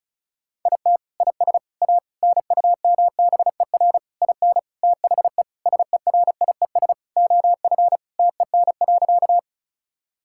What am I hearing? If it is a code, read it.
Morse code, heard as a keyed tone, about 34 wpm, 700 Hz. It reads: IT IS A NUMBER IN THE SERIES OF TEN.
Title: Sentences from Top 1000 Words - Review Part 5 - 35wpm